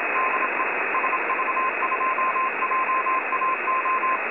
Morse code, heard as a keyed tone, about 25 wpm, 1040 Hz, with radio noise. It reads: DE SV2JAO
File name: websdr recording 70175.0kHz